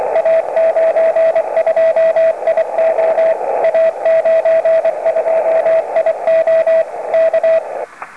Morse code, heard as a keyed tone, about 24 wpm, 650 Hz, with radio noise. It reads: A92IO A92IO K
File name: iw0ffk / 2012 05 19 A92IO rec by IK1EGC